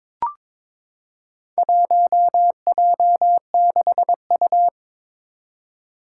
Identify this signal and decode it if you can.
Morse code, heard as a keyed tone, about 22 wpm, 700 Hz. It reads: 1J6U
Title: Four Challenging Letter-Number Combinations 22wpm